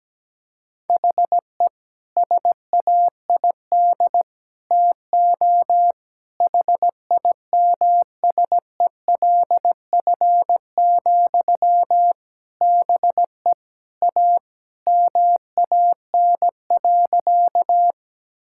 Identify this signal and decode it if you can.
Morse code, heard as a keyed tone, about 17 wpm, 700 Hz. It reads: HE SAID TO HIMSELF, BE A MAN.
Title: Sentences from Top 1600 Words 17wpm